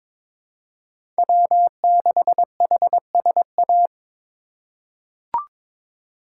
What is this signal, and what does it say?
Morse code, heard as a keyed tone, about 21 wpm, 700 Hz. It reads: W6HSA
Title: USA Call Signs - 5 Characters - 4x Character Spacing - 22wpm